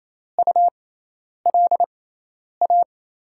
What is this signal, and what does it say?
Morse code, heard as a keyed tone, about 28 wpm, spaced as for 10 wpm, 700 Hz. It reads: ULA
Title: USA Call Signs - 6 Characters - 6x Character Spacing - 28wpm